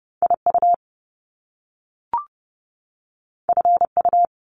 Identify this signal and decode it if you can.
Morse code, heard as a keyed tone, about 30 wpm, 700 Hz. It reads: IU FU
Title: Top Two Letter Combinations in Words 30wpm